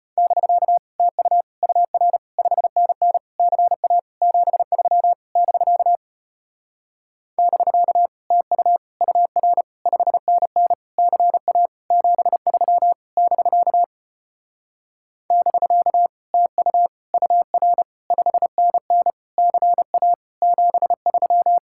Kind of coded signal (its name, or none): Morse code